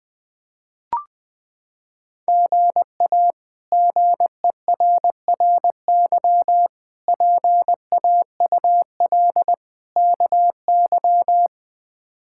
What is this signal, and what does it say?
Morse code, heard as a keyed tone, about 20 wpm, 700 Hz. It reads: GA GERRY PAUL KY
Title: K1USN SST QSO Practice - 20wpm